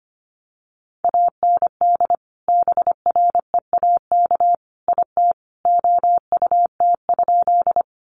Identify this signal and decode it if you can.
Morse code, heard as a keyed tone, about 25 wpm, 700 Hz. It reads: AND BREAK IT OUT?
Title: Sentences from Top 900 Words 25wpm